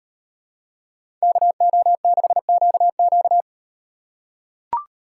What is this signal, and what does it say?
Morse code, heard as a keyed tone, about 35 wpm, 700 Hz. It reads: KO6QQ